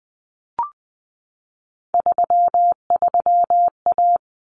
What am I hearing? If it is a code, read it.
Morse code, heard as a keyed tone, about 20 wpm, 700 Hz. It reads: 33A
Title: Common Cut Numbers - 3-Digit 20wpm